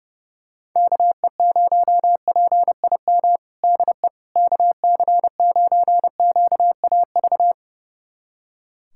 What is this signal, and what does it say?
Morse code, heard as a keyed tone, about 30 wpm, 690 Hz. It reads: KE0PIM DE KC9QAV
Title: USA Callsign DE Callsign - 6 Characters - 30wpm